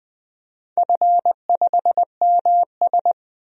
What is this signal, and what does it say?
Morse code, heard as a keyed tone, about 20 wpm, 700 Hz. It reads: F5MS
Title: DXCC-100 Call Signs - 4 Characters - 20wpm